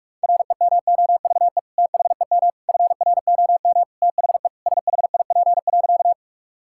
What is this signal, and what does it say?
Morse code, heard as a keyed tone, about 41 wpm, 700 Hz. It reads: REMOVE THEM FROM THE SHIP.
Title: Sentences from Top 1000 Words - Review Part 5 - 45wpm